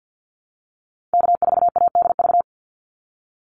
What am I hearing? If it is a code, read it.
Morse code, heard as a keyed tone, about 50 wpm, 700 Hz. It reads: K4ADV
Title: NA SOTA Activator Callsigns 50wpm